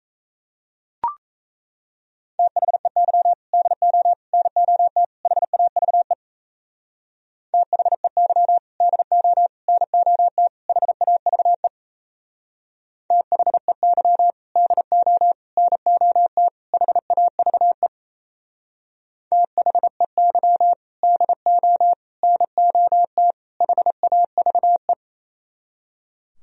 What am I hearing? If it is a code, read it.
Morse code, heard as a keyed tone, about 36 wpm, 700 Hz. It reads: THEY DO NOT HAVE THEY DO NOT HAVE THEY DO NOT HAVE THEY DO NOT HAVE